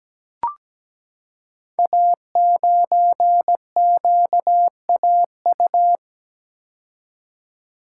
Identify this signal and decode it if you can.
Morse code, heard as a keyed tone, about 17 wpm, 700 Hz. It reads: A9QAU
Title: Code Groups - 5 Letter:Numbers - 17wpm